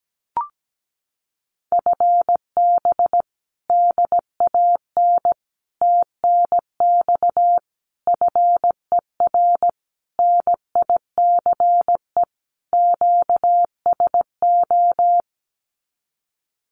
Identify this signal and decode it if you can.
Morse code, heard as a keyed tone, about 17 wpm, 700 Hz. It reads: FB DAN TNX FER NICE QSO
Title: Common QSO Phrases Encore 17wpm